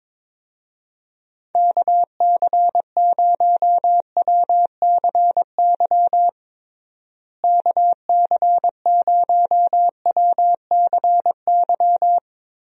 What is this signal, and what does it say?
Morse code, heard as a keyed tone, about 22 wpm, 700 Hz. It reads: KC0WCY KC0WCY
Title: USA Call Signs - 6 Characters - Repeated 3x - 22wpm